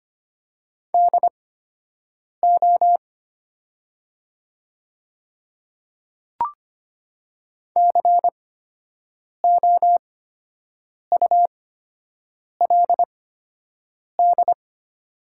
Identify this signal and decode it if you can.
Morse code, heard as a keyed tone, about 25 wpm, spaced as for 7 wpm, 690 Hz. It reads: DO COULD
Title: 8x Character Spacing - Top 200 Words - 25wpm